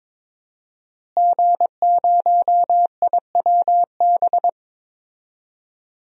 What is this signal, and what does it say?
Morse code, heard as a keyed tone, about 22 wpm, 700 Hz. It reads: G0IWB
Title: DXCC-100 Call Signs - 5 Characters - Repeated 2x - 22wpm